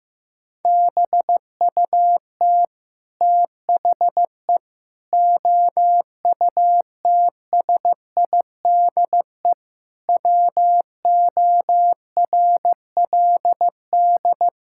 Morse code, heard as a keyed tone, about 15 wpm, 700 Hz. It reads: BUT THE OUTSIDE WORLD
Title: Sentences from Top 1000 Words - Review Part 6 - 15wpm